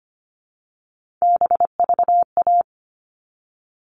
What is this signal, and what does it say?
Morse code, heard as a keyed tone, about 25 wpm, 700 Hz. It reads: BVA
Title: All Cut Numbers - 3-Digit 25wpm